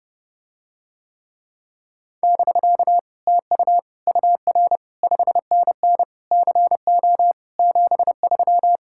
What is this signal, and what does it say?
Morse code, heard as a keyed tone, about 30 wpm, 700 Hz. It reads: <BK> TU UR 5NN CO 73